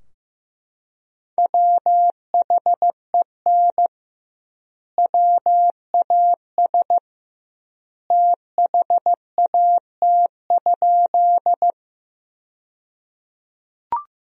Morse code, heard as a keyed tone, about 15 wpm, 700 Hz. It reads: WHEN WAS THAT?
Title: 2x Word-Spacing - Sentences from Top 100 Words - 15wpm